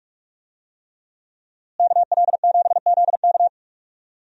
Morse code, heard as a keyed tone, about 41 wpm, 700 Hz. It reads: KL7ZK
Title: USA Call Signs - 5 Characters - Repeated 3x - 45wpm